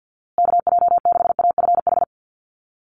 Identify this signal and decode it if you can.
Morse code, heard as a keyed tone, about 49 wpm, 700 Hz. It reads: KJ6AFH
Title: USA Call Signs Repeated 3x 50wpm